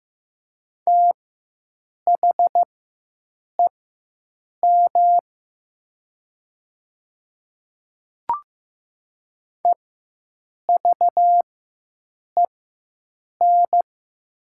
Morse code, heard as a keyed tone, about 15 wpm, spaced as for 7 wpm, 700 Hz. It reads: THEM EVEN
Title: 4x Character Spacing - Top 200 Words - 15wpm